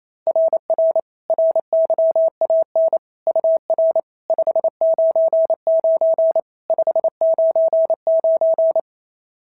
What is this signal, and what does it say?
Morse code, heard as a keyed tone, about 28 wpm, 660 Hz. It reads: RR RYAN UR 599 599
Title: SOTA QSO Practice 28wpm